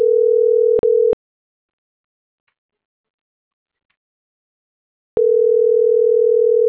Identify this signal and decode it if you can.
Ringback tone (North America), 440 + 480 Hz.